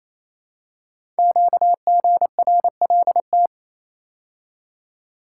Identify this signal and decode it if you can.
Morse code, heard as a keyed tone, about 29 wpm, 700 Hz. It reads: QGRLT
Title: Code Groups - 5 Letters - 28wpm